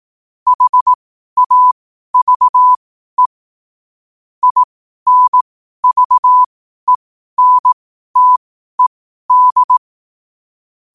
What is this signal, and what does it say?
Morse code, heard as a keyed tone, about 18 wpm, spaced as for 12 wpm, 1000 Hz. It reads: HAVE INVENTED
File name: qotdfiles / 12wpm